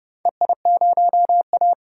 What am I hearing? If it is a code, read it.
Morse code, heard as a keyed tone, about 30 wpm, 700 Hz. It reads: EI0A
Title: DXCC-100 Call Signs - 4 Characters - 30wpm